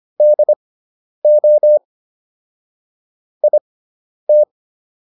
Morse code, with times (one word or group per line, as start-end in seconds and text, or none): 0.2-1.8 DO
3.4-4.4 IT